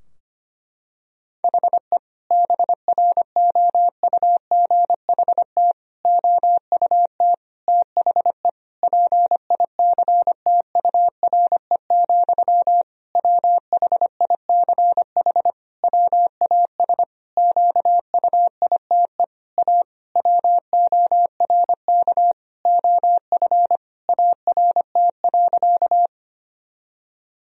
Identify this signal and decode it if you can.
Morse code, heard as a keyed tone, about 25 wpm, 700 Hz. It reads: HE BROUGHT OUT THE PICTURE, WHICH WAS QUITE A WORK OF ART.
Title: Sentences from Top 1000 Words - Review Part 3 - 25wpm